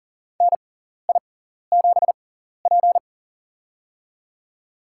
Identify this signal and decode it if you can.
Morse code, heard as a keyed tone, about 40 wpm, spaced as for 14 wpm, 700 Hz. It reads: NI7P
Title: USA Call Signs - 4 Characters - 6x Character Spacing - 40wpm